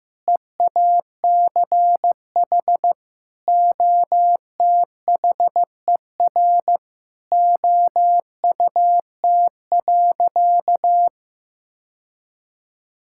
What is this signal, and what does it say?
Morse code, heard as a keyed tone, about 15 wpm, 700 Hz. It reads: EACH OTHER OUT.
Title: Sentences from Top 1000 Words - Review Part 6 - 15wpm